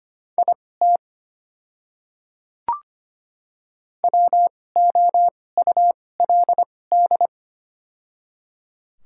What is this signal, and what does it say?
Morse code, heard as a keyed tone, about 25 wpm, spaced as for 18 wpm, 700 Hz. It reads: IT WOULD
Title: 2x Character Spacing - Top 100 Words - 25wpm